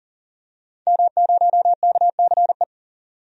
Morse code, heard as a keyed tone, about 40 wpm, 700 Hz. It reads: M0KCE
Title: DXCC-100 Call Signs - 5 Characters - 40wpm